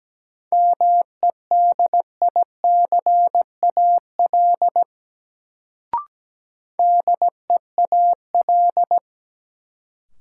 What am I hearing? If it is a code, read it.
Morse code, heard as a keyed tone, about 17 wpm, 700 Hz. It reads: MEDICAL DEAL
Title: Sentences from Top 800 Words - New Word Drill - 17wpm